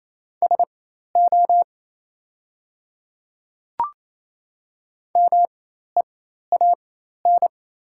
Morse code, heard as a keyed tone, about 28 wpm, spaced as for 13 wpm, 690 Hz. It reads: SO MEAN
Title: 4x Character Spacing - Top 200 Words - 28wpm